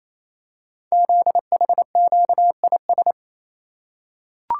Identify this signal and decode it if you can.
Morse code, heard as a keyed tone, about 27 wpm, 700 Hz. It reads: ZHQIS
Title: Code Groups - 5 Letters - 28wpm